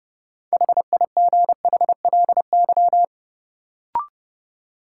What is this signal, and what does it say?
Morse code, heard as a keyed tone, about 30 wpm, 700 Hz. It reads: HIGHLY
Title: Sentences from Top 1100 Words - New Word Drill Alt - 30wpm